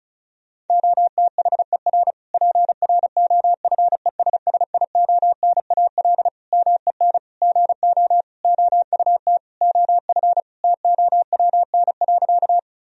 Morse code, heard as a keyed tone, about 36 wpm, 700 Hz. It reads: OTHER PROFESSIONAL MEN GO OUT OF TOWN.
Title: Sentences from Top 1000 Words - Review Part 4 - 35wpm